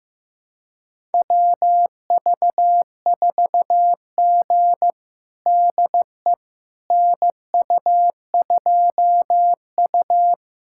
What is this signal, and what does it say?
Morse code, heard as a keyed tone, about 15 wpm, 700 Hz. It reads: WV4G DE NU2U
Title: USA Callsign DE Callsign - 4 Characters - 15wpm